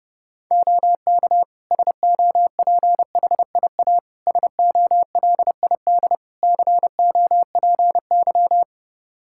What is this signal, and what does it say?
Morse code, heard as a keyed tone, about 30 wpm, 700 Hz. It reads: OK SOPHIA SOLID COPY
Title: Practice QSO - Exchange 3 - More Info - 30wpm